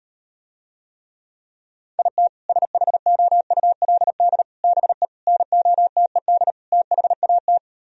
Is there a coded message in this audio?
Yes: Morse code from 2.0 to 7.6 s, heard as a keyed tone, about 38 wpm, 700 Hz. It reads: IT SHOULD BE NOTED THAT